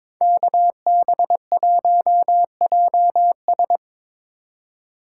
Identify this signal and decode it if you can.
Morse code, heard as a keyed tone, about 21 wpm, 700 Hz. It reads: KB1JS
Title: USA Call Signs - 2x Character Spacing - 22wpm